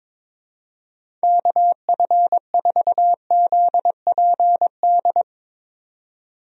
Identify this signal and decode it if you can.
Morse code, heard as a keyed tone, about 22 wpm, 700 Hz. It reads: KF4ZPD
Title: USA Call Signs - 6 Characters - 22wpm